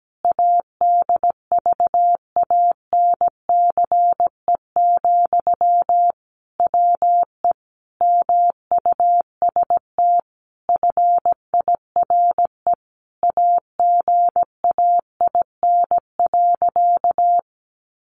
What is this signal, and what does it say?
Morse code, heard as a keyed tone, about 17 wpm, 700 Hz. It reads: ADVANCE, WE MUST FIRE AGAIN.
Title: Sentences from Top 1400 Words 17wpm